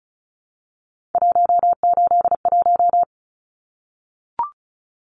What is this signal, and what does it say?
Morse code, heard as a keyed tone, about 34 wpm, 700 Hz. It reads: 181
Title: Numbers - 3-Digit 35wpm